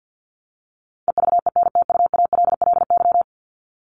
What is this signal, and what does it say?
Morse code, heard as a keyed tone, about 50 wpm, 700 Hz. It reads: EVENTUALLY
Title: Top 751-1000 Words 50wpm